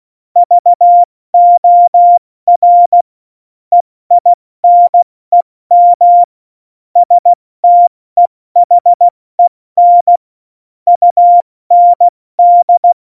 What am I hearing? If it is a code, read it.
Morse code, heard as a keyed tone, about 16 wpm, 700 Hz. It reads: VOR EINEM STEHEN UND
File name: Morsekurs / Witze / Witz 20. November 2018, 70 F80